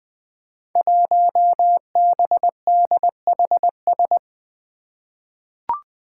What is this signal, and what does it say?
Morse code, heard as a keyed tone, about 21 wpm, 700 Hz. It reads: 1BDHS
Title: Five Challenging Letter-Number Combinations 20wpm